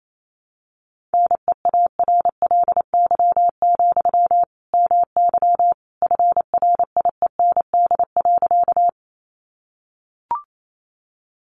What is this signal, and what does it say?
Morse code, heard as a keyed tone, about 28 wpm, 700 Hz. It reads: NEARLY, MY FRIEND.
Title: Sentences from Top 1300 Words 28wpm